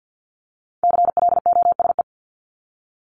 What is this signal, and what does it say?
Morse code, heard as a keyed tone, about 50 wpm, 700 Hz. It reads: CLOSE
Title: LY Suffix - Top 100 Words with Base Words 50wpm